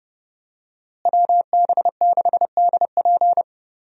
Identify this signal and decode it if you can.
Morse code, heard as a keyed tone, about 30 wpm, 700 Hz. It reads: WB6DP
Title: USA Call Signs - 5 Characters - Repeated 2x - 30wpm